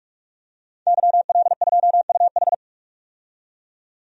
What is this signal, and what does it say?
Morse code, heard as a keyed tone, about 46 wpm, 700 Hz. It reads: YL2UH
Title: DXCC-100 Call Signs - 5 Characters - 45wpm